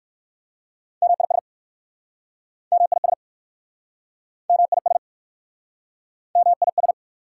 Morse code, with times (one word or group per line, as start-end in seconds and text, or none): 1.0-1.4 MIS
2.7-3.1 MIS
4.5-5.0 MIS
6.4-6.9 MIS